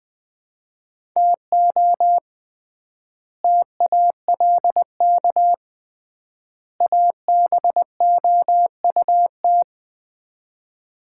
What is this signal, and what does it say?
Morse code, heard as a keyed tone, about 20 wpm, 700 Hz. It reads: TO TALK ABOUT
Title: 3x Word-Spacing - Sets of 3 Words - 20wpm